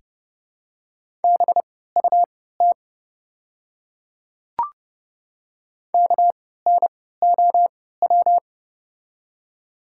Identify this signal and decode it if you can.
Morse code, heard as a keyed tone, about 30 wpm, spaced as for 17 wpm, 700 Hz. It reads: BUT KNOW
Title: 3x Character Spacing - Top 300 Words - 30wpm